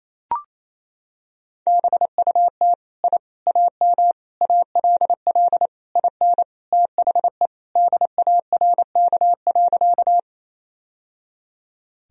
Morse code, heard as a keyed tone, about 28 wpm, 700 Hz. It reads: BUT I AM ALL IN THE DARK.
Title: Sentences from Top 1000 Words - Review Part 2 - 28wpm